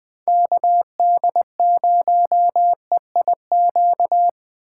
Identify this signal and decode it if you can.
Morse code, heard as a keyed tone, about 19 wpm, 700 Hz. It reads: KD0EIQ